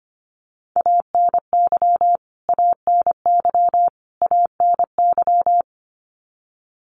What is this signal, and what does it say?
Morse code, heard as a keyed tone, about 25 wpm, 700 Hz. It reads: ANY ANY ANY
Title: Top 500 Words Repeated 25wpm